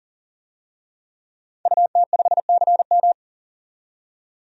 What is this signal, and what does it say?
Morse code, heard as a keyed tone, about 40 wpm, 700 Hz. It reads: UT5CM
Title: DXCC-100 Call Signs - 5 Characters - 40wpm